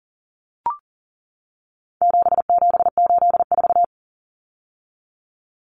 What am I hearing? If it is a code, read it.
Morse code, heard as a keyed tone, about 38 wpm, 700 Hz. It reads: 7784
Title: Common Cut Numbers - 4-Digit 40wpm